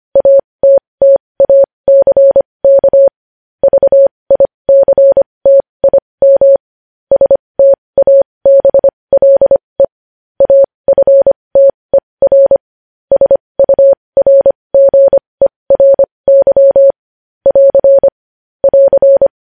Morse code, heard as a keyed tone, about 25 wpm, spaced as for 20 wpm, 550 Hz. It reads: ATTACK VICTIM STABLE AFTER SURGERY <AR> <AR>
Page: news headlines in morse code at 20 wpm